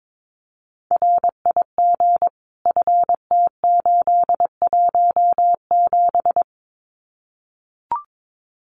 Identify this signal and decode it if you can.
Morse code, heard as a keyed tone, about 22 wpm, 700 Hz. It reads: RIG FT817
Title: Common QSO Phrases Encore 22wpm